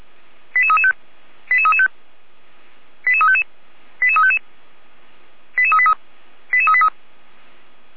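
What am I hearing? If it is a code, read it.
Five-tone selective call (ZVEI), heard as quick runs of five tones, tones of about 80 ms. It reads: 88335 88335 88355 88355 88383 88383
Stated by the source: ZVEI five-tone call received by SDR.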